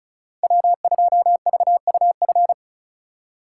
Morse code, heard as a keyed tone, about 36 wpm, 700 Hz. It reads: W2VUF
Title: K1USN SST QSO Practice - 35wpm